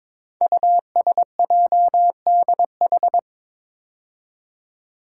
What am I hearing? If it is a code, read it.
Morse code, heard as a keyed tone, about 22 wpm, 700 Hz. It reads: USJDH
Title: Code Groups - 5 Letters - 22wpm